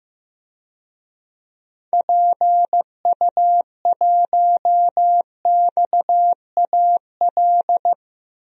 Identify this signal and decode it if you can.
Morse code, heard as a keyed tone, about 16 wpm, 700 Hz. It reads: PU1XAL
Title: DXCC-100 Call Signs - 6 Characters - Repeated 3x - 15wpm